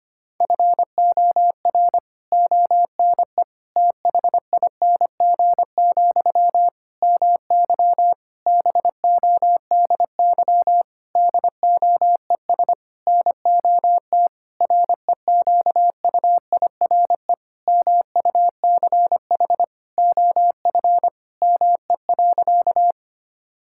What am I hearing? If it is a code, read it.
Morse code, heard as a keyed tone, about 25 wpm, 700 Hz. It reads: FOR ONE THING, MY BODY DOES NOT REQUIRE MUCH OF ME.
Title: Sentences from Top 1000 Words - Review Part 6 - 25wpm